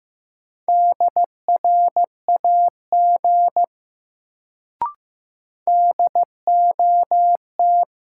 Morse code, heard as a keyed tone, about 15 wpm, 700 Hz. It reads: DRAG DOT
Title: Sentences from Top 1500 Words - New Word Drill - 15wpm